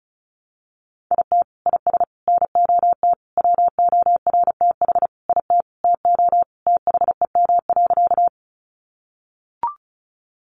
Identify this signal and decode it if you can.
Morse code, heard as a keyed tone, about 34 wpm, 700 Hz. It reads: IT IS NOT WORTH IT TO THEM.
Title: Sentences from Top 800 Words 35wpm